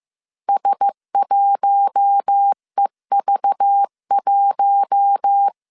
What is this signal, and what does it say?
Morse code, heard as a keyed tone, about 15 wpm, 800 Hz. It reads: S1EV1